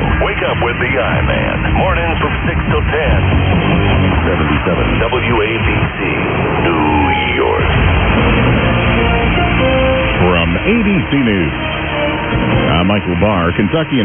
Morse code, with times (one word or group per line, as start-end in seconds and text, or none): none